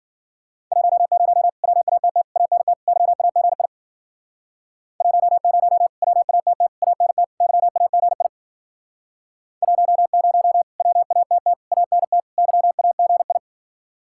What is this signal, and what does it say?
Morse code, heard as a keyed tone, about 53 wpm, 700 Hz. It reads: 10 WATT ANT YAGI 10 WATT ANT YAGI 10 WATT ANT YAGI